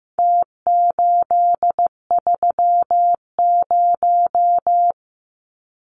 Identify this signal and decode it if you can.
Morse code, heard as a keyed tone, about 15 wpm, 700 Hz. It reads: T830
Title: Numbers - Contest Style - 4-Digit 15wpm